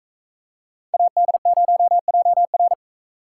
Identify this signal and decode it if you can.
Morse code, heard as a keyed tone, about 39 wpm, 700 Hz. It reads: AD0JR